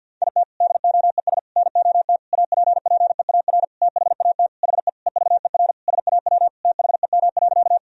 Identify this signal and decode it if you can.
Morse code, heard as a keyed tone, about 48 wpm, 700 Hz. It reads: IT DOES NOT APPEAR THAT HE EVER SAW THEM.
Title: Sentences from Top 1000 Words - Review Part 2 - 50wpm